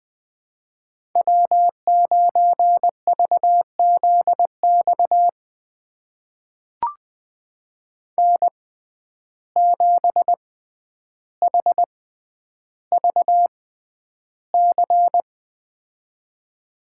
Morse code, heard as a keyed tone, about 20 wpm, spaced as for 10 wpm, 700 Hz. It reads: W9VZX N7HVC
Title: USA Call Signs - 5 Characters - 6x Character Spacing - 20wpm